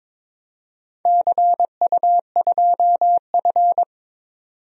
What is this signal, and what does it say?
Morse code, heard as a keyed tone, about 22 wpm, 700 Hz. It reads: CU2F
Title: DXCC-100 Call Signs - 4 Characters - 22wpm